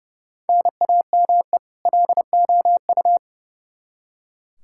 Morse code, heard as a keyed tone, about 30 wpm, 700 Hz. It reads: NAME LOU
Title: Practice QSO - Exchange 1 - Beginning - 30wpm